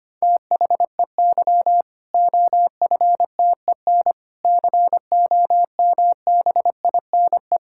Morse code, heard as a keyed tone, about 25 wpm, 700 Hz. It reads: THEY OFTEN COMBINE
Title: Sentences from Top 1600 Words 25wpm